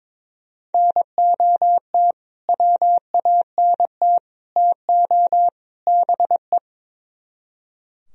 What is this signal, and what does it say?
Morse code, heard as a keyed tone, about 22 wpm, 700 Hz. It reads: NOT WANT TO BE
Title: Speed-Racing - Sets of 4 Words 22wpm